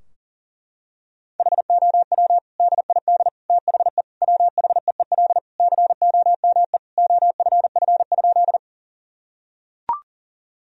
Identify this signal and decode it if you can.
Morse code, heard as a keyed tone, about 40 wpm, 700 Hz. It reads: HOW DID THE WHEEL COME OFF?
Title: Sentences from Top 1000 Words - Review Part 3 - 40wpm